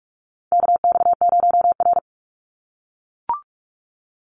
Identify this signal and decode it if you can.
Morse code, heard as a keyed tone, about 44 wpm, 700 Hz. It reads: KX0R
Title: NA SOTA Chaser Callsigns 45wpm